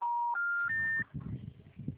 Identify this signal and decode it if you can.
Special information tone.